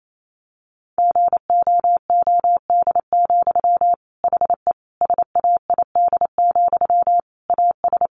Morse code, heard as a keyed tone, about 28 wpm, 700 Hz. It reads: GOOD, HE SAID, AS
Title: Sentences from Top 600 Words 28wpm